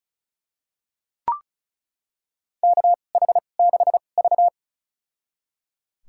Morse code, heard as a keyed tone, about 36 wpm, spaced as for 25 wpm, 700 Hz. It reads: KH6V
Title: USA Call Signs - 4 Characters - 2x Character Spacing - 35wpm